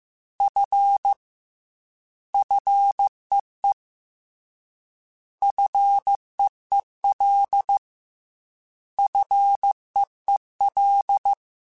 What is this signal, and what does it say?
Morse code, heard as a keyed tone, about 15 wpm, 800 Hz. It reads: F FEE FEEL FEEL
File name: Combinations / Combinations-mp3s / Combinations@13